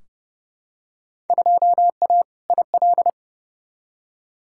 Morse code, heard as a keyed tone, about 30 wpm, 700 Hz. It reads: 2A IL
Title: Field Day Exchanges - 30wpm